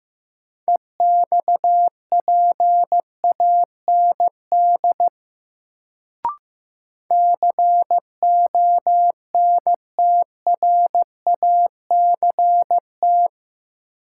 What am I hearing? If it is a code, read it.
Morse code, heard as a keyed tone, about 15 wpm, 700 Hz. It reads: EXPAND CONTRACT
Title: Sentences from Top 1400 Words - New Word Drill Alt - 15wpm